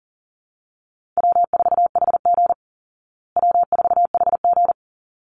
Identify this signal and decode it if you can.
Morse code, heard as a keyed tone, about 40 wpm, 700 Hz. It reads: W4HG W4HG
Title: Call Signs Repeated 3x - 40wpm